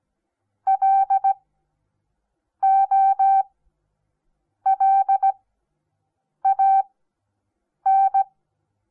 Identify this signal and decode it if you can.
Morse code, heard as a keyed tone, about 17 wpm, spaced as for 6 wpm, 780 Hz. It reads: LOLAN
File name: Onlinelehrgang / morsen / DJ4UF Morsen08